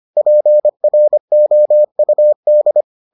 Morse code, heard as a keyed tone, about 25 wpm, 600 Hz. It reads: PROUD